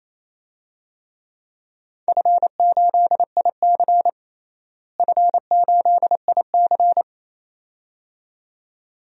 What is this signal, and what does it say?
Morse code, heard as a keyed tone, about 28 wpm, 700 Hz. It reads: F8IC F8IC
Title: DXCC-100 Call Signs - 4 Characters - Repeated 2x - 28wpm